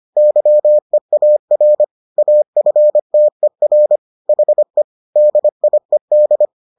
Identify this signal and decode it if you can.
Morse code, heard as a keyed tone, about 25 wpm, 600 Hz. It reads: YEAR AFTER HE DIED